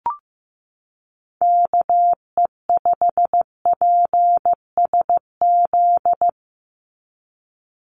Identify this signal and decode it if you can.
Morse code, heard as a keyed tone, about 15 wpm, 700 Hz. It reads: KE5PSZ
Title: USA Call Signs 15wpm